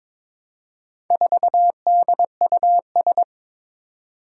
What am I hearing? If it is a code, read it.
Morse code, heard as a keyed tone, about 22 wpm, 700 Hz. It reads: 4DUS
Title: Four Challenging Letter-Number Combinations 22wpm